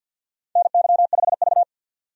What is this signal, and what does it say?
Morse code, heard as a keyed tone, about 52 wpm, 700 Hz. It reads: NY5V
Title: USA Call Signs - 4 Characters - 50wpm